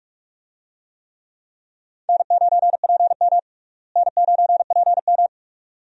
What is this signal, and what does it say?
Morse code, heard as a keyed tone, about 46 wpm, 700 Hz. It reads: N9PM N9PM
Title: USA Call Signs - 4 Characters - Repeated 2x - 45wpm